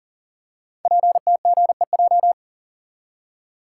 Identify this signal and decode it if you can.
Morse code, heard as a keyed tone, about 42 wpm, 690 Hz. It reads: PTGEJ
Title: Code Groups - 5 Letters - 40wpm